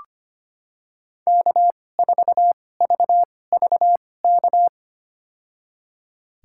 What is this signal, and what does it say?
Morse code, heard as a keyed tone, about 25 wpm, spaced as for 18 wpm, 700 Hz. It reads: K4VVK
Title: USA Call Signs - 5 Characters - 2x Character Spacing - 25wpm